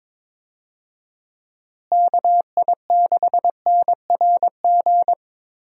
Morse code, heard as a keyed tone, about 22 wpm, 700 Hz. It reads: KI6NRG
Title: USA Call Signs - 6 Characters - Repeated 3x - 22wpm